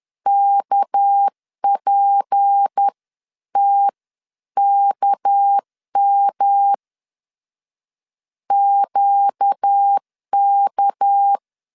Morse code, heard as a keyed tone, about 10 wpm, spaced as for 9 wpm, 800 Hz. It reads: KPTKM QK
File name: LEZIONI / 6 PKX0 / 6 PK 10wpm 2436